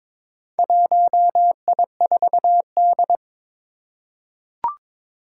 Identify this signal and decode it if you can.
Morse code, heard as a keyed tone, about 21 wpm, 700 Hz. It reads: 1I4D
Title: Four Challenging Letter-Number Combinations 22wpm